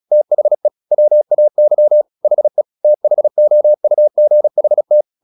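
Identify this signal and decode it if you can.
Morse code, heard as a keyed tone, about 36 wpm, 600 Hz. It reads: THE WAY HE THOUGHT
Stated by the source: Morse practice at 36 wpm, text of Ken Follett's novel Never, from CW med Gnister.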